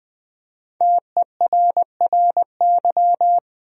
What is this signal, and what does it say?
Morse code, heard as a keyed tone, about 20 wpm, 700 Hz. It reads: TERRY
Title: USA Common Names - 20wpm